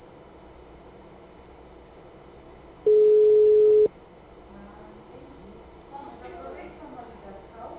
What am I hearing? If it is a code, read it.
Ringback tone (Europe, 425 Hz), with speech.
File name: out- -2014- - - .68983